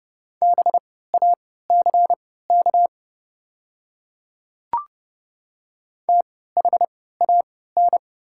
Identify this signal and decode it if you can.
Morse code, heard as a keyed tone, about 30 wpm, spaced as for 17 wpm, 700 Hz. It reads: BACK THAN
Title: 3x Character Spacing - Top 300 Words - 30wpm